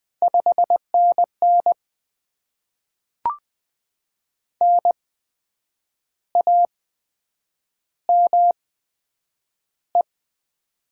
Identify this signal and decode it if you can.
Morse code, heard as a keyed tone, about 20 wpm, spaced as for 8 wpm, 700 Hz. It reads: 5NN NAME
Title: Top 20 QSO Elements - 8x Character Spacing - 20wpm